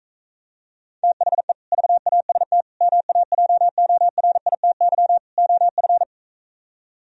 Morse code, heard as a keyed tone, about 45 wpm, 700 Hz. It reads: THE VAST MAJORITY OF